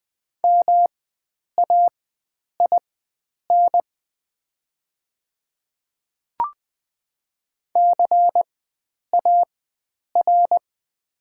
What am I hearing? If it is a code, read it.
Morse code, heard as a keyed tone, about 20 wpm, spaced as for 9 wpm, 700 Hz. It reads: MAIN CAR
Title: 4x Character Spacing - Top 300 Words - 20wpm